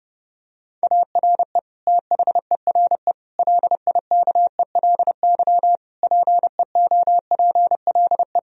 Morse code, heard as a keyed tone, about 30 wpm, 700 Hz. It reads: ARE THERE LIKELY PEOPLE
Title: Sentences from Top 1000 Words - Review Part 4 - 30wpm